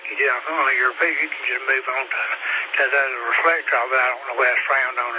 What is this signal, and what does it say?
Emergency Alert System (SAME) tones, heard as harsh buzzing bursts, with speech. It is heard from 1.0 to 4.9 s.